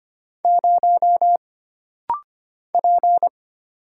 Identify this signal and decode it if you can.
Morse code, heard as a keyed tone, about 25 wpm, 700 Hz. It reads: 0 P
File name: Single Character (Letter:Number:Punctuation:Prosign) - Rapid-Fire - 25wpm